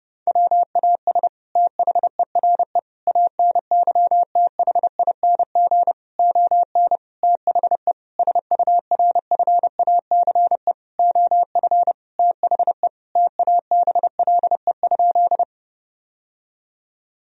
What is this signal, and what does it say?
Morse code, heard as a keyed tone, about 30 wpm, 700 Hz. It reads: WAS THERE ANYTHING ON THE SURFACE OF THE TABLE?
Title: Sentences from Top 1000 Words - Review Part 3 - 30wpm